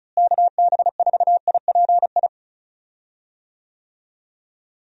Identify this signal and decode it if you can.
Morse code, heard as a keyed tone, about 34 wpm, 700 Hz. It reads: KB4IPI
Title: USA Call Signs - 6 Characters - Repeated 2x - 35wpm